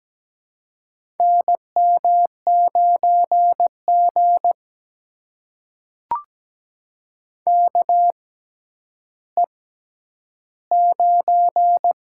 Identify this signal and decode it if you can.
Morse code, heard as a keyed tone, about 17 wpm, spaced as for 10 wpm, 700 Hz. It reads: NM9G KE9
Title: USA Call Signs - 6x Character Spacing - 17wpm